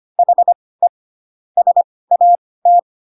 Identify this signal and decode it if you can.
Morse code, heard as a keyed tone, about 25 wpm, spaced as for 18 wpm, 700 Hz. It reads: HE SAT
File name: Pinocchio / Pinocchio ch6 12 25 ik1hge